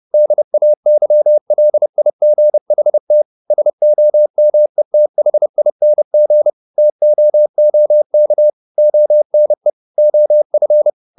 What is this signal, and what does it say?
Morse code, heard as a keyed tone, about 30 wpm, 600 Hz. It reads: DAYLIGHT SOMETHING TOOK ONE OF
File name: the old man and the sea / the old man and the sea 1 30wpm